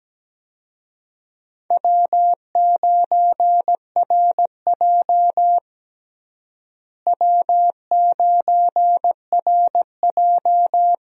Morse code, heard as a keyed tone, about 17 wpm, 700 Hz. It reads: W9RJ W9RJ
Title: USA Call Signs - 4 Characters - Repeated 3x - 17wpm